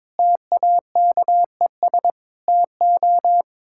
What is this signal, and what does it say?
Morse code, heard as a keyed tone, about 22 wpm, 700 Hz. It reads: TAKES TO
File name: practice / 22wpm / Sets of 7 Words - 22wpm